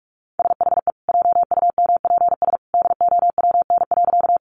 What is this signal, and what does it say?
Morse code, heard as a keyed tone, about 44 wpm, 700 Hz. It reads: SHE JUMPS DOWN.
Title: Easy Phrases 45wpm